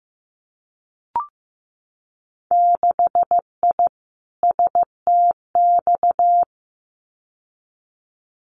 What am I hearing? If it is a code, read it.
Morse code, heard as a keyed tone, about 15 wpm, 700 Hz. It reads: 6I STX
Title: Winter Field Day Exchanges Encore 15wpm